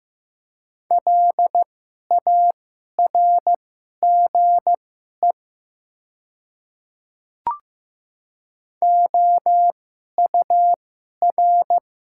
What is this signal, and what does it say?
Morse code, heard as a keyed tone, about 15 wpm, spaced as for 11 wpm, 700 Hz. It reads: LARGE OUR
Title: 2x Character Spacing - Top 200 Words - 15wpm